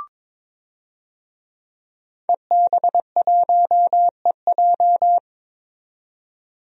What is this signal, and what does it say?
Morse code, heard as a keyed tone, about 22 wpm, 700 Hz. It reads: EB1EJ
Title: DXCC-100 Call Signs - 5 Characters - 22wpm